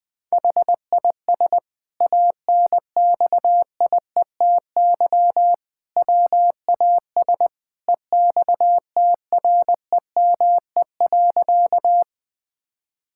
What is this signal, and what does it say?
Morse code, heard as a keyed tone, about 20 wpm, 700 Hz. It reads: HIS ANXIETY WAS EXTREME.
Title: Sentences from Top 1600 Words 20wpm